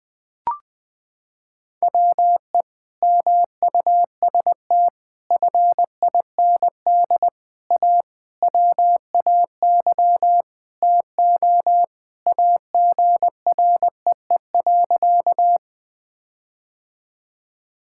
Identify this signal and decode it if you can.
Morse code, heard as a keyed tone, about 20 wpm, 700 Hz. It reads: WE MUST FIND A WAY TO AGREE.
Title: Sentences with 5-Word Phrases - 20wpm